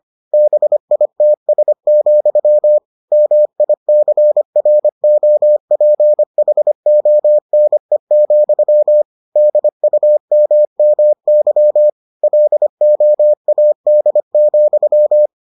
Morse code, heard as a keyed tone, about 25 wpm, 600 Hz. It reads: BITS, MICROPHONE, DUMMY LOAD,